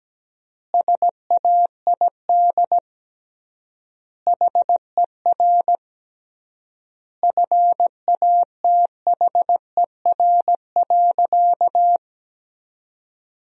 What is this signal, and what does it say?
Morse code, heard as a keyed tone, about 17 wpm, 700 Hz. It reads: SAID HER FATHER.
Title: 3x Word-Spacing - Sentences from Top 300 Words - 17wpm